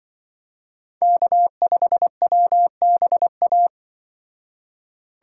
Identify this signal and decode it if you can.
Morse code, heard as a keyed tone, about 25 wpm, 700 Hz. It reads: K5WBA
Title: Speed Racing - USA 1x3 Call Signs 20wpm